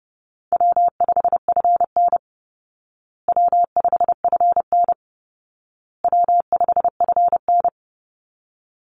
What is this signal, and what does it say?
Morse code, heard as a keyed tone, about 30 wpm, 700 Hz. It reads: W5FN W5FN W5FN
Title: Easy Call Signs Repeated 3x - 30wpm